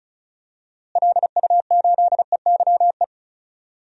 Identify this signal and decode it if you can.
Morse code, heard as a keyed tone, about 34 wpm, 700 Hz. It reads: LU8EYE
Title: DXCC-100 Call Signs - 6 Characters - Repeated 2x - 35wpm